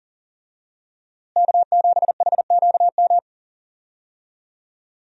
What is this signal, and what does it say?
Morse code, heard as a keyed tone, about 40 wpm, 700 Hz. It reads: K7HQM
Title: USA Call Signs - 5 Characters - Repeated 2x - 40wpm